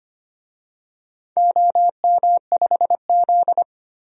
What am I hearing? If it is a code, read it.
Morse code, heard as a keyed tone, about 25 wpm, 700 Hz. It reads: OM5Z
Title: DXCC-100 Call Signs - 4 Characters - 25wpm